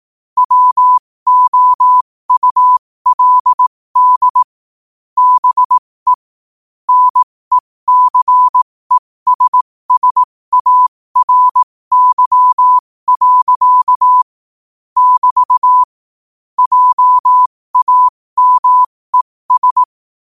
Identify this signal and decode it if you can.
Morse code, heard as a keyed tone, about 18 wpm, spaced as for 15 wpm, 1000 Hz. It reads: WOULD BE NECESSARY. <BT> JAMES
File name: qotdfiles / 15wpm